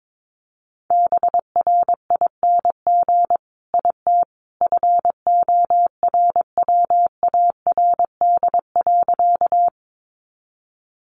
Morse code, heard as a keyed tone, about 22 wpm, 700 Hz. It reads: BRING IT FORWARD.
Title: Sentences from Top 1300 Words 22wpm